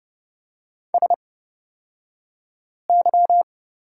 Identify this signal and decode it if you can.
Morse code, heard as a keyed tone, about 30 wpm, spaced as for 5 wpm, 700 Hz. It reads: SY
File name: Lesson 36d - BT - Two Characters 30wpm